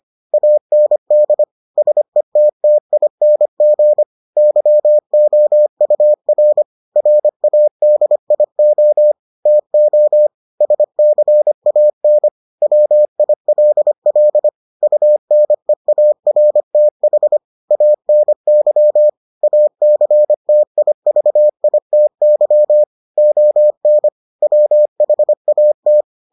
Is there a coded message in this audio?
Yes: Morse code from 0.3 to 26.0 s, heard as a keyed tone, about 25 wpm, 600 Hz. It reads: AND SETTING YOUR RADIO TO SCAN WILL UNEARTH ANY ACTIVITY ON WHAT